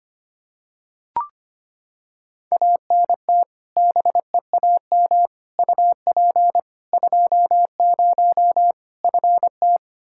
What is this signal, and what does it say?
Morse code, heard as a keyed tone, about 25 wpm, 700 Hz. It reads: ANT BEAM UP 20 FT
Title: K1USN SST QSO Practice 25wpm